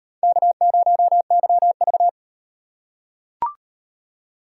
Morse code, heard as a keyed tone, about 35 wpm, 700 Hz. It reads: K0YV